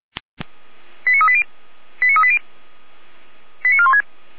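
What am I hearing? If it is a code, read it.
Five-tone selective call (ZVEI), heard as quick runs of five tones, tones of about 70 ms. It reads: 89380 89380 89416
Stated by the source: Five-tone ZVEI selective call received by SDR, fire brigade (Feuerwehr) Orlamünde.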